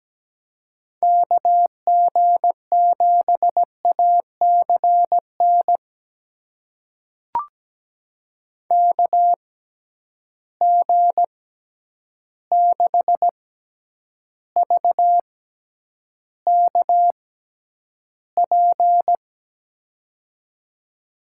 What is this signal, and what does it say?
Morse code, heard as a keyed tone, about 17 wpm, spaced as for 9 wpm, 700 Hz. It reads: KG7ACN KG6VKP
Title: USA Call Signs - 6 Characters - 6x Character Spacing - 17wpm